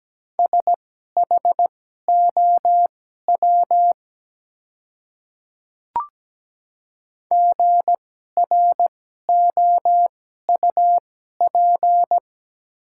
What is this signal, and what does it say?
Morse code, heard as a keyed tone, about 17 wpm, spaced as for 12 wpm, 700 Hz. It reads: SHOW GROUP
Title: 2x Character Spacing - Top 300 Words - 17wpm